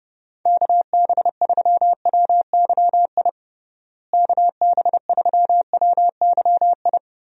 Morse code, heard as a keyed tone, about 32 wpm, 700 Hz. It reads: KB3WYI KB3WYI
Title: USA Call Signs - 6 Characters - Repeated 3x - 30wpm